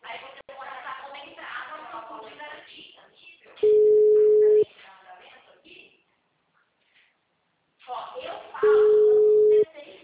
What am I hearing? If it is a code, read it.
Ringback tone (Europe, 425 Hz), with speech.